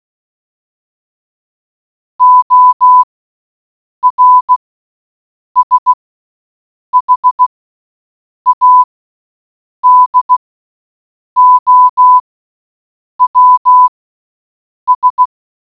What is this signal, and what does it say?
Morse code, heard as a keyed tone, about 16 wpm, spaced as for 7 wpm, 1000 Hz. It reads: OR SHADOWS